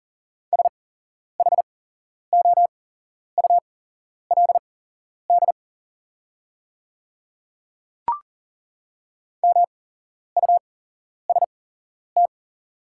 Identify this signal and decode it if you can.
Morse code, heard as a keyed tone, about 42 wpm, spaced as for 11 wpm, 700 Hz. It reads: SHOULD MUST
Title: 8x Character Spacing - Top 300 Words - 40wpm